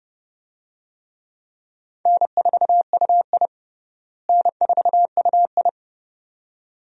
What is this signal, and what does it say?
Morse code, heard as a keyed tone, about 32 wpm, 700 Hz. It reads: N4UI N4UI
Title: USA Call Signs - 4 Characters - Repeated 2x - 30wpm